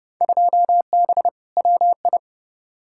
Morse code, heard as a keyed tone, about 30 wpm, 700 Hz. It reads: 2B WI
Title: Field Day Exchanges - 30wpm